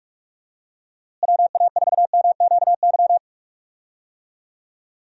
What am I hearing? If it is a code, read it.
Morse code, heard as a keyed tone, about 46 wpm, 700 Hz. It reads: WA4MQY
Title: USA Call Signs - 6 Characters - Repeated 3x - 45wpm